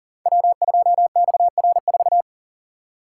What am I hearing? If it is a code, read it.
Morse code, heard as a keyed tone, about 40 wpm, 700 Hz. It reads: W2XR4
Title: Code Groups - 5 Letter:Numbers - 40wpm